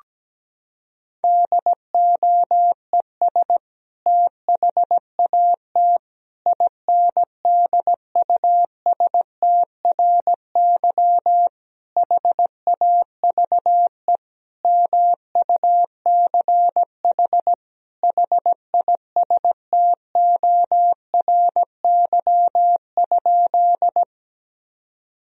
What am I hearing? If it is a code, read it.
Morse code, heard as a keyed tone, about 17 wpm, 700 Hz. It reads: DOES THAT INDUSTRY HAVE MUCH HISTORY?
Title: Sentences from Top 1000 Words - Review Part 4 - 17wpm